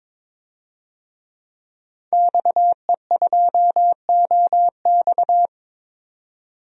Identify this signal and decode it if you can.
Morse code, heard as a keyed tone, about 21 wpm, 700 Hz. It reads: XE2OX
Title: DXCC-100 Call Signs - 5 Characters - 22wpm